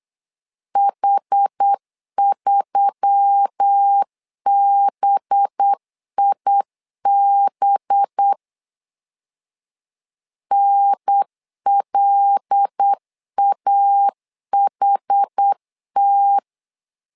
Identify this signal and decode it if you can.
Morse code, heard as a keyed tone, about 8 wpm, 800 Hz. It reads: H3BIB NLAHT
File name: LEZIONI / 5 LBVC / 5 LB 8wpm 2816